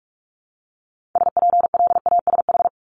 Morse code, heard as a keyed tone, about 46 wpm, 700 Hz. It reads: SPLASH
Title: Multi-letter Phonemes 45wpm